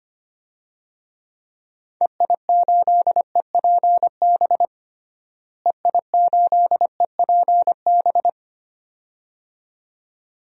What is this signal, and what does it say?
Morse code, heard as a keyed tone, about 25 wpm, 700 Hz. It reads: EI8EPB EI8EPB
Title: DXCC-100 Call Signs - 6 Characters - Repeated 2x - 25wpm